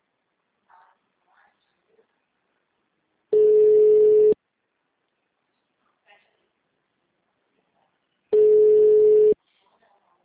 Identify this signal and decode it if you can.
Ringback tone (Europe, 425 Hz), with speech.